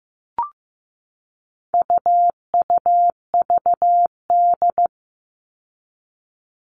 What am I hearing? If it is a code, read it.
Morse code, heard as a keyed tone, about 15 wpm, 700 Hz. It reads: UUVD
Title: All Cut Numbers - 4-Digit 15wpm